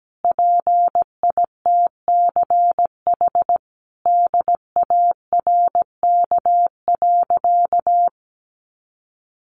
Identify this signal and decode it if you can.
Morse code, heard as a keyed tone, about 17 wpm, 700 Hz. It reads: PITCH DARK.
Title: Sentences from Top 1300 Words 17wpm